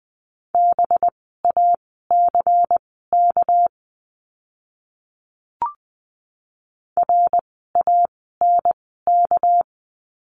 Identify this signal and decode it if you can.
Morse code, heard as a keyed tone, about 20 wpm, spaced as for 14 wpm, 700 Hz. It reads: BACK RANK
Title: 199 Lesson 32f - Letter k - Words 20wpm